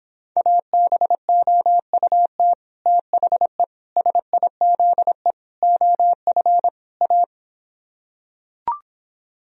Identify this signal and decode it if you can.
Morse code, heard as a keyed tone, about 26 wpm, 700 Hz. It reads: ABOUT THE SIZE OF A